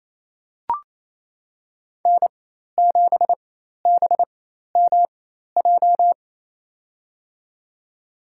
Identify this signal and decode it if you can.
Morse code, heard as a keyed tone, about 28 wpm, spaced as for 13 wpm, 690 Hz. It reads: N7BMJ
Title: USA Call Signs - 5 Characters - 4x Character Spacing - 28wpm